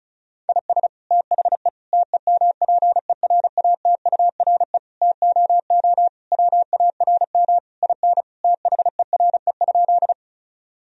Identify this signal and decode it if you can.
Morse code, heard as a keyed tone, about 36 wpm, 700 Hz. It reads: IS THE TEMPERATURE TOO WARM IN THERE?
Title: Sentences from Top 1000 Words - Review Part 4 - 35wpm